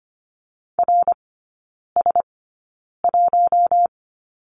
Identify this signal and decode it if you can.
Morse code, heard as a keyed tone, about 25 wpm, spaced as for 9 wpm, 700 Hz. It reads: RS1
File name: Lesson 17e - Letter u - Three Characters 25wpm